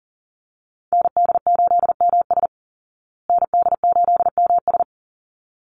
Morse code, heard as a keyed tone, about 40 wpm, 700 Hz. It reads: ND8MS ND8MS
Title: Call Signs Repeated 3x - 40wpm